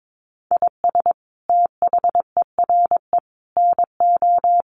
Morse code, heard as a keyed tone, about 22 wpm, 700 Hz. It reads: IS THERE NO
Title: Sentences from Top 1500 Words 22wpm